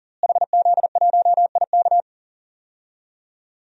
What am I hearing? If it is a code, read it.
Morse code, heard as a keyed tone, about 40 wpm, 700 Hz. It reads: HZ1IK
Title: DXCC-100 Call Signs - 5 Characters - Repeated 3x - 40wpm